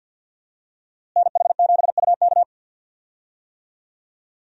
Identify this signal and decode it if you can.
Morse code, heard as a keyed tone, about 53 wpm, 700 Hz. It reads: NH7UK
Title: USA Call Signs - 5 Characters - Repeated 2x - 50wpm